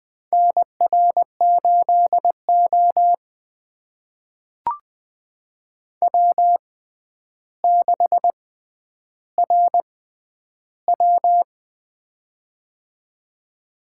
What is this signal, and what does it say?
Morse code, heard as a keyed tone, about 20 wpm, spaced as for 10 wpm, 700 Hz. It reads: NR8O W6RW
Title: USA Call Signs - 6x Character Spacing - 20wpm